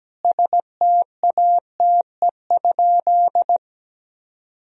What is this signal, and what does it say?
Morse code, heard as a keyed tone, about 17 wpm, 700 Hz. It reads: STATE?
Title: Sentences from Top 1000 Words - Review Part 2 - 17wpm